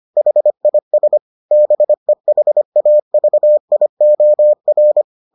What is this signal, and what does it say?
Morse code, heard as a keyed tone, about 25 wpm, 600 Hz. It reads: HIS BEHAVIOR